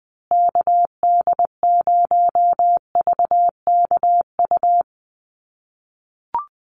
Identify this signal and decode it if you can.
Morse code, heard as a keyed tone, about 20 wpm, 700 Hz. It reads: KD0VKU
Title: USA Call Signs Repeated 2x 20wpm